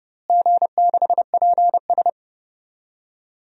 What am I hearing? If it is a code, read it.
Morse code, heard as a keyed tone, about 30 wpm, 700 Hz. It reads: G6PS
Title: DXCC-100 Call Signs - 4 Characters - Repeated 2x - 30wpm